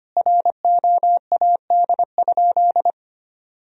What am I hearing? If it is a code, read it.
Morse code, heard as a keyed tone, about 25 wpm, 700 Hz. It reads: ROAD?
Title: Sentences from Top 1000 Words - Review Part 2 - 25wpm